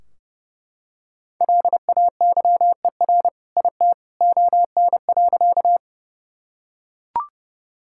Morse code, heard as a keyed tone, about 30 wpm, 700 Hz. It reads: LAYER IT ON.
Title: Sentences from Top 1600 Words 30wpm